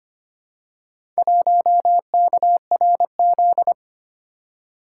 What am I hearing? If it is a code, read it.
Morse code, heard as a keyed tone, about 25 wpm, 700 Hz. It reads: 1KRZ
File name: practice / 25wpm / Four Letter-Number Combinations - 25wpm